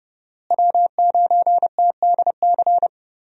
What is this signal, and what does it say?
Morse code, heard as a keyed tone, about 30 wpm, 700 Hz. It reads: W9TDC
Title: USA Call Signs - 5 Characters - Repeated 2x - 30wpm